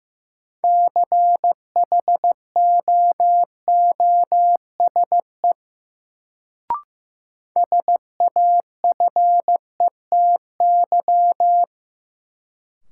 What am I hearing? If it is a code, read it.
Morse code, heard as a keyed tone, about 15 wpm, 700 Hz. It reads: CHOOSE SAFETY
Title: Sentences from Top 700 Words - New Word Drill - 15wpm